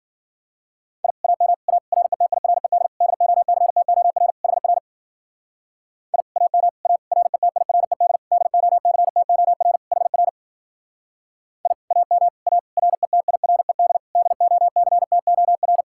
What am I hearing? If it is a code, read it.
Morse code, heard as a keyed tone, about 50 wpm, 700 Hz. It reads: I AM A RETIRED DOCTOR HR I AM A RETIRED DOCTOR HR I AM A RETIRED DOCTOR
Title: Speed-Racing Common QSO Phrases - 40wpm